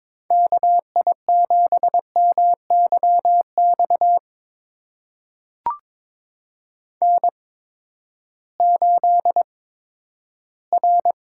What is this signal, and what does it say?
Morse code, heard as a keyed tone, about 22 wpm, spaced as for 12 wpm, 700 Hz. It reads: KI7MYX N8R
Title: USA Call Signs - 8x Character Spacing - 22wpm